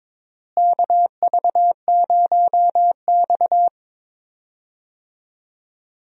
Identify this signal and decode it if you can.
Morse code, heard as a keyed tone, about 22 wpm, 700 Hz. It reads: KV0X
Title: USA Call Signs - 4 Characters - Repeated 2x - 22wpm